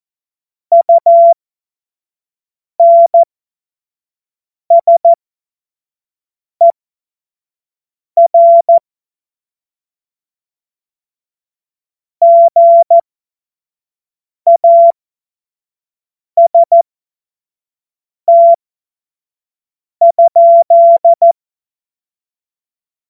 Morse code, heard as a keyed tone, about 14 wpm, spaced as for 5 wpm, 680 Hz. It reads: UNSER GAST?